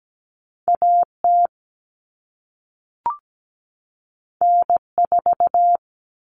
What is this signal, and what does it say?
Morse code, heard as a keyed tone, about 17 wpm, 700 Hz. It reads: AT N4
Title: All Cut Numbers - 2-Digit 17wpm